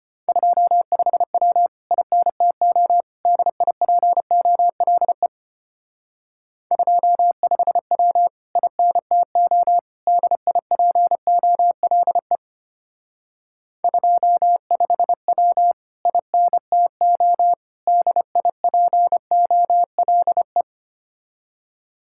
Morse code, heard as a keyed tone, about 31 wpm, 700 Hz. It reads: 25W INTO DIPOLE 25W INTO DIPOLE 25W INTO DIPOLE
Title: Speed-Racing Common QSO Phrases - 25wpm